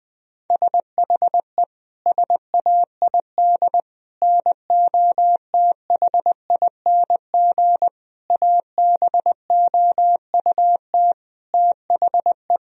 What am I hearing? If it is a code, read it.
Morse code, heard as a keyed tone, about 20 wpm, 700 Hz. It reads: SHE SAID NOTHING ABOUT THE
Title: Sentences from Top 1000 Words - Review Part 2 - 20wpm